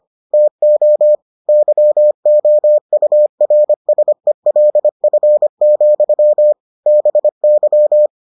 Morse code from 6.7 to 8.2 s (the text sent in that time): BY